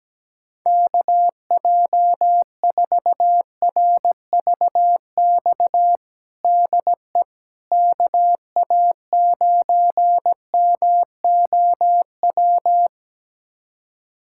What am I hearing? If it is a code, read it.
Morse code, heard as a keyed tone, about 17 wpm, 700 Hz. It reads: KJ4RVX DE KA9MOW
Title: USA Callsign DE Callsign - 6 Characters - 17wpm